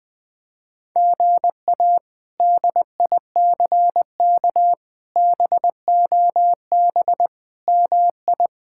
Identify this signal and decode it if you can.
Morse code, heard as a keyed tone, about 20 wpm, 700 Hz. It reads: GA DICK BOB MI
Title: K1USN SST QSO Practice - 20wpm